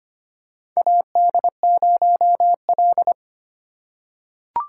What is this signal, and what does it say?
Morse code, heard as a keyed tone, about 25 wpm, 700 Hz. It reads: AD0L
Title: USA Call Signs - 4 Characters - 4x Character Spacing - 25wpm